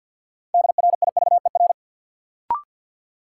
Morse code, heard as a keyed tone, about 46 wpm, 700 Hz. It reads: DRIVER
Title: Sentences from Top 1000 Words - New Word Drill - 50wpm